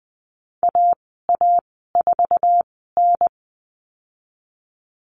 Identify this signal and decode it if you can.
Morse code, heard as a keyed tone, about 20 wpm, spaced as for 14 wpm, 700 Hz. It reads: AA4N
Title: 062 Lesson 11g - Letter h - Call Signs 20wpm